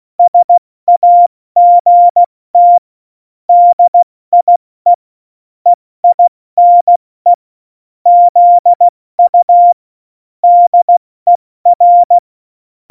Morse code, heard as a keyed tone, about 16 wpm, 700 Hz. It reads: SAGT DIE EINE ZU DER
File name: Morsekurs / Witze / Witz 10.Juli 2018, 70 F80